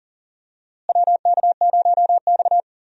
Morse code, heard as a keyed tone, about 40 wpm, 700 Hz. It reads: WK0X
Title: USA Call Signs - 4 Characters - Repeated 2x - 40wpm